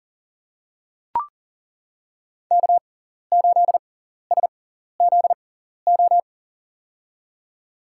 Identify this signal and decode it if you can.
Morse code, heard as a keyed tone, about 40 wpm, spaced as for 14 wpm, 700 Hz. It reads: K8SZO
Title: USA Call Signs - 5 Characters - 6x Character Spacing - 40wpm